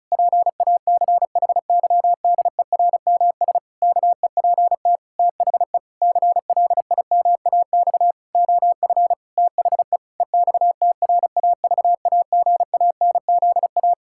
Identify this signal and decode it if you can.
Morse code, heard as a keyed tone, about 36 wpm, 700 Hz. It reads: PACHYDERMS KEPT THE CLIMAX OF THE EXTRAVAGANZA
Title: Pangram Challenge v2 - 35wpm